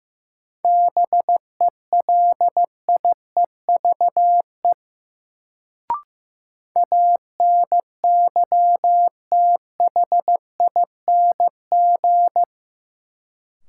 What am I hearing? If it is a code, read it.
Morse code, heard as a keyed tone, about 15 wpm, 700 Hz. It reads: BELIEVE ANYTHING
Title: Sets of 4 Words - New Word Drill - 15wpm